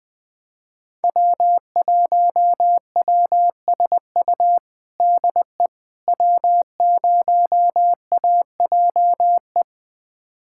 Morse code, heard as a keyed tone, about 20 wpm, 700 Hz. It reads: W1WSU DE W0AJE
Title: USA Callsign DE Callsign - 5 Characters - 20wpm